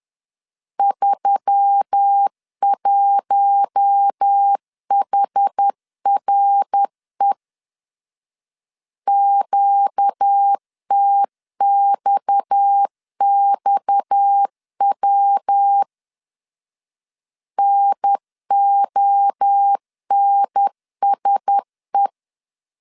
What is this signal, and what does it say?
Morse code, heard as a keyed tone, about 10 wpm, 800 Hz. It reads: 31HRE QTXXW NONSE
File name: CW / DET / 1 6 1500C S B 10wpm 4137